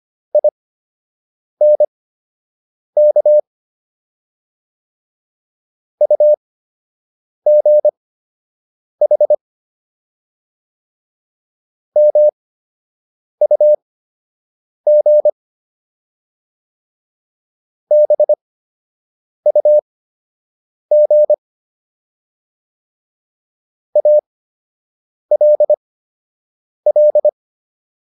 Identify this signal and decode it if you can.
Morse code, heard as a keyed tone, about 25 wpm, spaced as for 7 wpm, 600 Hz. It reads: INK UGH MUG BUG ALL